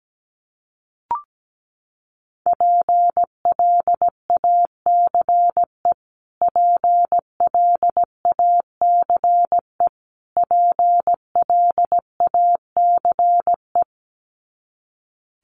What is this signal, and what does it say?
Morse code, heard as a keyed tone, about 17 wpm, 700 Hz. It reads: PLACE PLACE PLACE
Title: Top 300 Words Repeated 17wpm